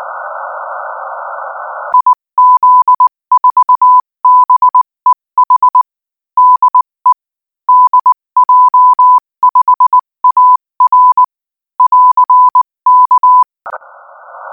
Morse code, heard as a keyed tone, about 20 wpm, 1000 Hz. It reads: IZ4BEH DE DJ5AR <AR>K
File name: IZ4BEH-13-cm-AS